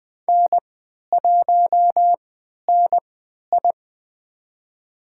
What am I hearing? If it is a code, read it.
Morse code, heard as a keyed tone, about 20 wpm, spaced as for 11 wpm, 700 Hz. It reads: N1NI
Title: USA Call Signs - 4 Characters - 3x Character Spacing - 20wpm